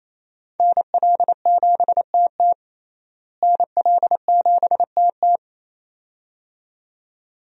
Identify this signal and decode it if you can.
Morse code, heard as a keyed tone, about 27 wpm, 700 Hz. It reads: NL7TT NL7TT
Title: USA Call Signs - 5 Characters - Repeated 2x - 28wpm